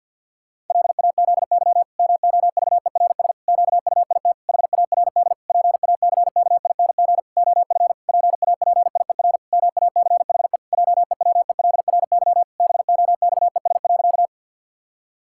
Morse code, heard as a keyed tone, about 50 wpm, 700 Hz. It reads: LAZY MOVERS QUIT HARD PACKING OF PAPIER MACHE JEWELRY BOXES.
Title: Pangram Challenge v2 - 50wpm